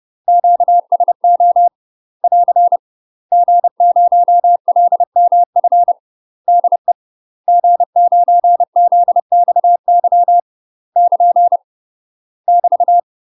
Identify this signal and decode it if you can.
Morse code, heard as a keyed tone, about 30 wpm, 700 Hz. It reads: QSO <AR> G0LMF DE G9ZXY <KN> <BT>
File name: UK MASTER QSOs TRAINING / 30WPM / UK MASTER QSOs 0041